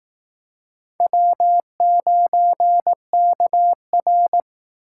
Morse code, heard as a keyed tone, about 19 wpm, 700 Hz. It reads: W9KR
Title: Speed Racing - USA 1x2 Call Signs 15wpm